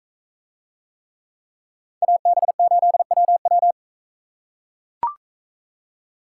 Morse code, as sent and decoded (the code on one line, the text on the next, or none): .- -... ---.. .-- .--
AB8WW